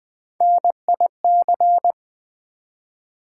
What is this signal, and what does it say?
Morse code, heard as a keyed tone, about 20 wpm, 700 Hz. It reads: NIC
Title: Build A Word - 1 Letter At A Time - 4-Letter Words - 20wpm